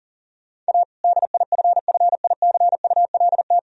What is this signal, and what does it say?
Morse code, heard as a keyed tone, about 40 wpm, 700 Hz. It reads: A DIFFICULT
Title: Sentences from Top 1000 Words - Review Part 2 - 40wpm